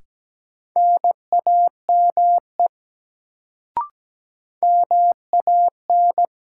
Morse code, heard as a keyed tone, about 17 wpm, 700 Hz. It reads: NAME MAN
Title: Sentences from Top 200 Words - New Word Drill - 17wpm